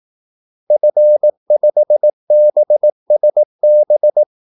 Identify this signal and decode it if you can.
Morse code, heard as a keyed tone, about 18 wpm, 600 Hz. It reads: F5BSB
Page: Kaldesignaler 18 wpm